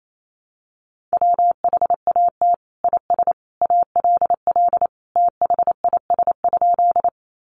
Morse code, heard as a keyed tone, about 28 wpm, 700 Hz. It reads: WHAT IS ALL THIS?
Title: Sentences from Top 100 Words 28wpm